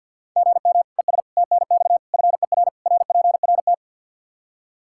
Morse code, heard as a keyed tone, about 52 wpm, 700 Hz. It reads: GM ES TNX FER RPRT
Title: Practice QSO - Exchange 1 - Beginning - 50wpm